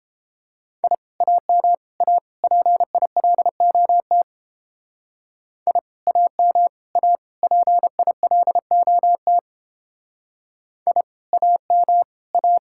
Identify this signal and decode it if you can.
Morse code, heard as a keyed tone, about 29 wpm, 700 Hz. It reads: I AM A PILOT I AM A PILOT I AM A